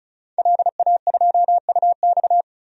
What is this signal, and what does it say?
Morse code, heard as a keyed tone, about 36 wpm, 700 Hz. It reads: LA2UX
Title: DXCC-100 Call Signs - 5 Characters - 35wpm